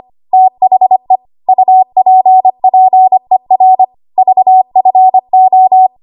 Morse code, heard as a keyed tone, about 25 wpm, 750 Hz. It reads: THE UPPER VFO
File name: Archive / 25 WPM / 25WPM